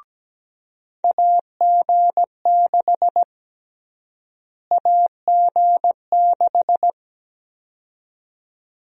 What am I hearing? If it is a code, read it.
Morse code, heard as a keyed tone, about 17 wpm, 700 Hz. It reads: AG6 AG6
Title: USA Call Sign Prefixes - Repeated 2x - 17wpm